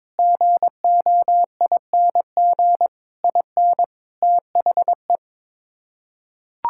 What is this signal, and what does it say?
Morse code, heard as a keyed tone, about 22 wpm, 700 Hz. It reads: GOING IN THE
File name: practice / 22wpm / Sets of 8 Words - 22wpm